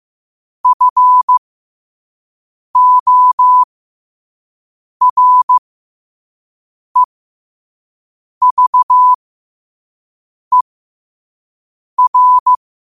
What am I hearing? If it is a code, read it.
Morse code, heard as a keyed tone, about 15 wpm, spaced as for 5 wpm, 1000 Hz. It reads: FOREVER
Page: New quotes every day in morse code at 5 Words per minute.